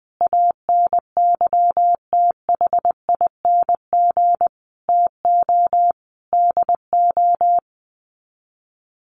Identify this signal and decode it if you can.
Morse code, heard as a keyed tone, about 20 wpm, 700 Hz. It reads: ANYTHING TO DO
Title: Sets of 5 Words Encore 20wpm